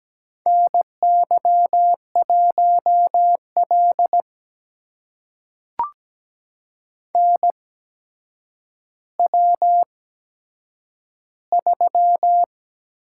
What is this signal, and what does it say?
Morse code, heard as a keyed tone, about 17 wpm, spaced as for 8 wpm, 700 Hz. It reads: NY1L NW3
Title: USA Call Signs - 4 Characters - 8x Character Spacing - 17wpm